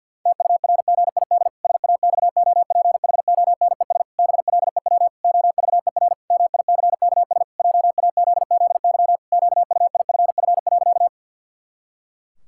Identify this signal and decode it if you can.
Morse code, heard as a keyed tone, about 50 wpm, 700 Hz. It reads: TURGID SAXOPHONES BLEW OVER MICKS JAZZY QUIFF.
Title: Pangram Challenge v2 - 50wpm